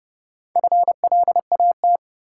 Morse code, heard as a keyed tone, about 32 wpm, 700 Hz. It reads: FLAT
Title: Sentences from Top 1000 Words - New Word Drill - 30wpm